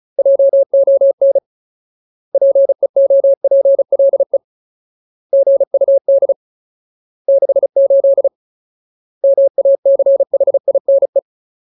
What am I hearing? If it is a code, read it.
Morse code, heard as a keyed tone, about 34 wpm, 550 Hz. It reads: JON PEOPLE GUD 68 MACHINE
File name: Ultimate Word List / 35WPM Ultimate